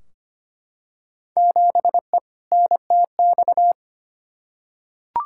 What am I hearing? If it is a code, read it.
Morse code, heard as a keyed tone, about 25 wpm, 700 Hz. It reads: 7E NTX
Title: Field Day Exchanges Encore - 25wpm